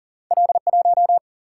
Morse code, heard as a keyed tone, about 40 wpm, 700 Hz. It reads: L1
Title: Single-Letter Single-Number 40wpm